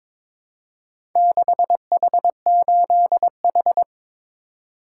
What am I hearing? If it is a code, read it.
Morse code, heard as a keyed tone, about 22 wpm, 700 Hz. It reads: 6H8H
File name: practice / 22wpm / Four Letter-Number Combinations - 22wpm